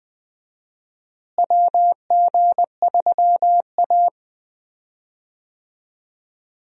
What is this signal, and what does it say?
Morse code, heard as a keyed tone, about 20 wpm, 700 Hz. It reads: WG3A
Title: USA Call Signs - 4 Characters - Repeated 3x - 20wpm